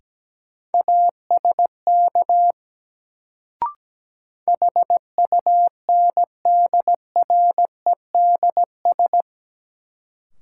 Morse code, heard as a keyed tone, about 17 wpm, 700 Hz. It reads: ASK HUNDREDS
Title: Sets of 4 Words - New Word Drill - 17wpm